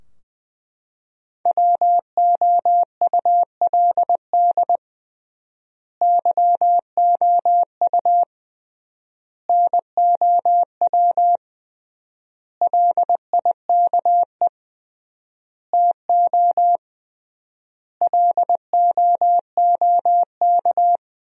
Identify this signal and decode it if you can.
Morse code, heard as a keyed tone, about 20 wpm, 700 Hz. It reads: WOULD YOU NOW LIKE TO LOOK
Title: 3x Word-Spacing - Sentences from Top 200 Words - 20wpm